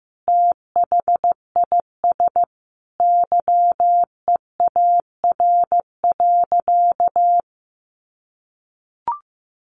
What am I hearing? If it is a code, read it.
Morse code, heard as a keyed tone, about 15 wpm, 700 Hz. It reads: THIS YEAR.
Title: Sentences from Top 1400 Words 15wpm